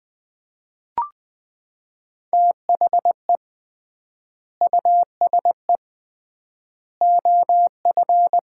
Morse code, heard as a keyed tone, about 20 wpm, 700 Hz. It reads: THE USE OF
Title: 3x Word-Spacing - Sets of 3 Words - 20wpm